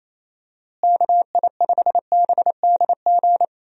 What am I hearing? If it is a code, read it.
Morse code, heard as a keyed tone, about 28 wpm, 700 Hz. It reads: KI5BDG
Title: USA Call Signs - 6 Characters - Repeated 2x - 28wpm